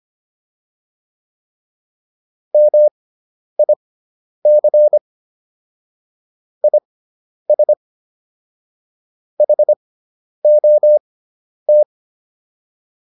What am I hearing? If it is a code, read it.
Morse code, heard as a keyed tone, about 25 wpm, spaced as for 10 wpm, 600 Hz. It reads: MIC IS HOT